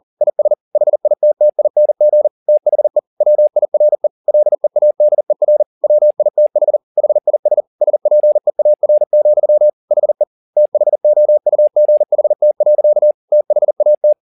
Morse code, heard as a keyed tone, about 40 wpm, 600 Hz. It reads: IS HITTING THE WIRE LEADER WITH HIS SPEAR, HE THOUGHT. THAT
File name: the old man and the sea 2 40wpm